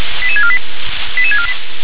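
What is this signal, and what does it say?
Five-tone selective call (ZVEI), heard as quick runs of five tones, tones of about 70 ms. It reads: 88648 88648
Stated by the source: ZVEI five-tone selective call received by SDR.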